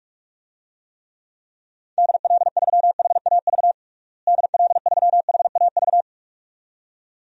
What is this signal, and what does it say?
Morse code, heard as a keyed tone, about 45 wpm, 700 Hz. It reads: DL3HAV DL3HAV
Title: DXCC-100 Call Signs - 6 Characters - Repeated 2x - 45wpm